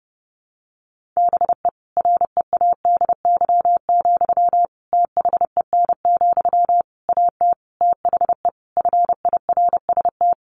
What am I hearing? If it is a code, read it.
Morse code, heard as a keyed tone, about 30 wpm, 700 Hz. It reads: BE READY, THEN, AT THE FIRST
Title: Sentences from Top 1400 Words 30wpm